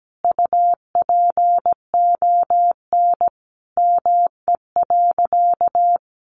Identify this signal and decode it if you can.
Morse code, heard as a keyed tone, about 17 wpm, 700 Hz. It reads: UPON ME.
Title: Sentences from Top 1100 Words 17wpm